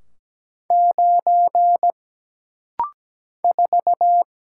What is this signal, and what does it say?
Morse code, heard as a keyed tone, about 17 wpm, 700 Hz. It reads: 9 4
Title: Single Letter-Number - Rapid-Fire 17wpm